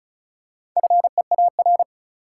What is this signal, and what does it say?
Morse code, heard as a keyed tone, about 36 wpm, 700 Hz. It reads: FEAR
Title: Four Letter Words - 35wpm